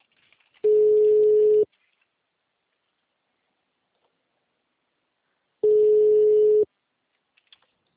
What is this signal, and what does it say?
Ringback tone (Europe, 425 Hz).